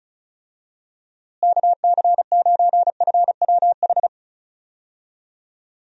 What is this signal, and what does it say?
Morse code, heard as a keyed tone, about 36 wpm, 700 Hz. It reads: KC9FWH
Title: USA Call Signs - 6 Characters - Repeated 2x - 35wpm